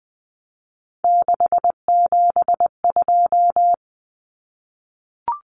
Morse code, heard as a keyed tone, about 20 wpm, 700 Hz. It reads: 672
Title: Numbers - 3-Digit 20wpm